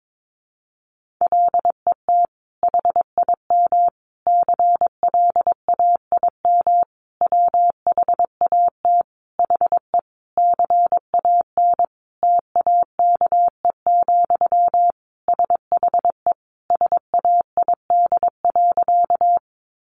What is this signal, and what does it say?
Morse code, heard as a keyed tone, about 22 wpm, 700 Hz. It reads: LET HIM CLAIM WHAT HE CAN TAKE, SHE SAID.
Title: Sentences from Top 1500 Words 22wpm